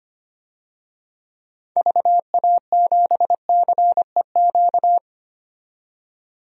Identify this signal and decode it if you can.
Morse code, heard as a keyed tone, about 25 wpm, 700 Hz. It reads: VA7CEQ
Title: DXCC-100 Call Signs - 6 Characters - 25wpm